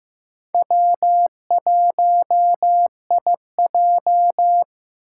Morse code, heard as a keyed tone, about 15 wpm, 700 Hz. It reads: W1IJ
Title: USA Call Signs - 2x Character Spacing - 15wpm